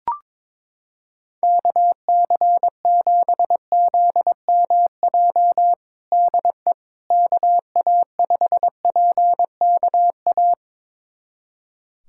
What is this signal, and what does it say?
Morse code, heard as a keyed tone, about 22 wpm, 700 Hz. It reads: KC7ZMJ DE KA5PKA
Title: USA Callsign DE Callsign - 6 Characters - 22wpm